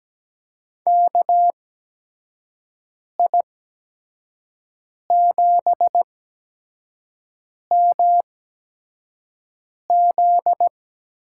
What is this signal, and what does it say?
Morse code, heard as a keyed tone, about 17 wpm, spaced as for 5 wpm, 700 Hz. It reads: KI7MZ
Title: USA Call Signs - 8x Character Spacing - 17wpm